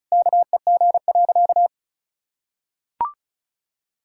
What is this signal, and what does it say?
Morse code, heard as a keyed tone, about 36 wpm, 700 Hz. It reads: KEG.
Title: Pangram Challenge v2 - 35wpm